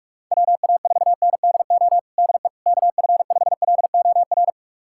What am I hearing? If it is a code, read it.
Morse code, heard as a keyed tone, about 46 wpm, 690 Hz. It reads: WA4NDO DE KF5LOR
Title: USA Callsign DE Callsign 45wpm